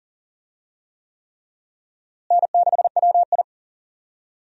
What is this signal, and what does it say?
Morse code, heard as a keyed tone, about 40 wpm, 700 Hz. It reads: N6WI
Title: USA Call Signs - 4 Characters - Repeated 3x - 40wpm